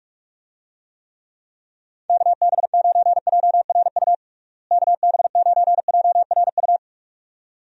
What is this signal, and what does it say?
Morse code, heard as a keyed tone, about 46 wpm, 700 Hz. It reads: KB9JRU KB9JRU
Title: USA Call Signs - 6 Characters - Repeated 2x - 45wpm